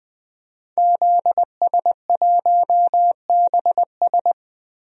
Morse code, heard as a keyed tone, about 20 wpm, 700 Hz. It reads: ZS1BS
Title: DXCC-100 Call Signs - 5 Characters - 20wpm